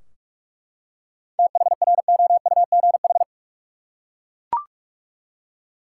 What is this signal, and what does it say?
Morse code, heard as a keyed tone, about 41 wpm, 700 Hz. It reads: THROUGH
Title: Seven Letter Words - 45wpm